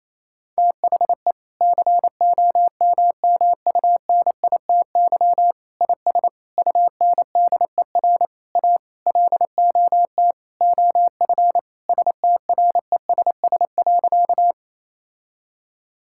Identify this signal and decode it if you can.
Morse code, heard as a keyed tone, about 28 wpm, 700 Hz. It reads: THE COMMUNITY IS UNDER A LOT OF STRESS.
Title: Sentences from Top 1000 Words - Review Part 6 - 28wpm